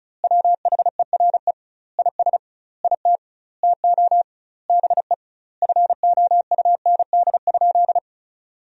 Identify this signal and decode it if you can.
Morse code, heard as a keyed tone, about 36 wpm, 700 Hz. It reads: WHERE IS IT TO BE FOUND?
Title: 2x Word-Spacing - Sentences from Top 300 Words - 35wpm